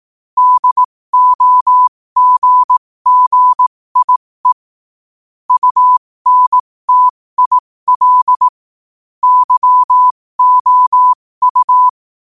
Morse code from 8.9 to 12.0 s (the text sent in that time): YOU